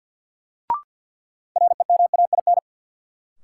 Morse code, heard as a keyed tone, about 49 wpm, 700 Hz. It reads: REMAIN
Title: Sentences from Top 1200 Words - New Word Drill Alt - 50wpm